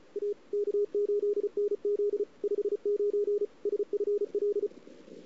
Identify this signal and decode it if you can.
Morse code, heard as a keyed tone, about 33 wpm, 410 Hz. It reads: A K8NZ 59 SFL
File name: CW / 2008-ss-cw-k5tr-op-n5rz -04:00